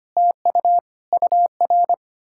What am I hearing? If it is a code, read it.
Morse code, heard as a keyed tone, about 33 wpm, 700 Hz. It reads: TU UR